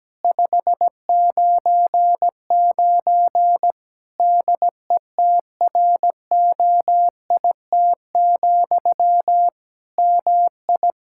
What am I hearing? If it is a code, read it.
Morse code, heard as a keyed tone, about 17 wpm, 700 Hz. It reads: 599 DETROIT, MI
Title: Practice QSO - Exchange 1 - Beginning - 17wpm